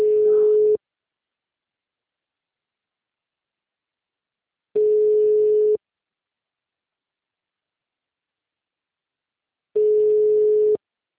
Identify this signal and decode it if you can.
Ringback tone (Europe, 425 Hz).